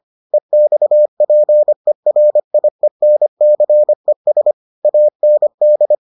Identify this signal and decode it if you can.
Morse code, heard as a keyed tone, about 25 wpm, 600 Hz. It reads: EXPERIENCES AND